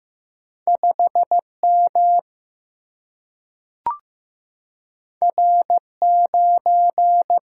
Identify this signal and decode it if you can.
Morse code, heard as a keyed tone, about 15 wpm, 700 Hz. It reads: 5 M R 9
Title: Single-Letter Single-Number 15wpm